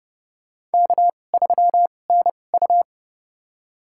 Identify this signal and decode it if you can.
Morse code, heard as a keyed tone, about 32 wpm, spaced as for 22 wpm, 700 Hz. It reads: K3NU
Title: USA Call Signs - 4 Characters - 2x Character Spacing - 30wpm